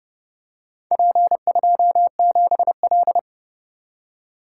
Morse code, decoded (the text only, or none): P27L